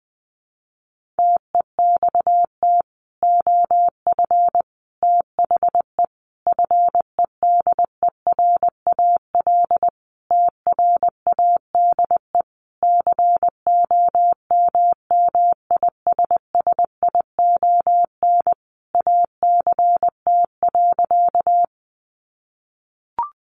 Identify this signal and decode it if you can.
Morse code, heard as a keyed tone, about 20 wpm, 700 Hz. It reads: TEXT OF THE FEDERAL TRADE COMMISSION ACT.
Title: Sentences from Top 1400 Words 20wpm